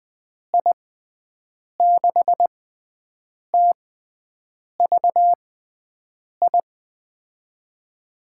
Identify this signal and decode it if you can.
Morse code, heard as a keyed tone, about 20 wpm, spaced as for 7 wpm, 700 Hz. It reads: I6TVI
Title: USA Call Signs - 6 Characters - 6x Character Spacing - 20wpm